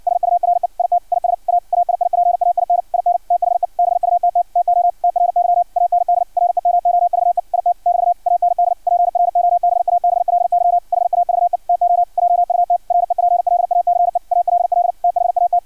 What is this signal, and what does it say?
Morse code, heard as a keyed tone, about 56 wpm, 700 Hz, with radio noise. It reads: SURE IT IS A AIIPAIN IN THE BUTT TO TRY AND REMOVE IT <BT> AND PROBABLY HAVE TO JUT REPLACE ALL THAT
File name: archived / CRSnet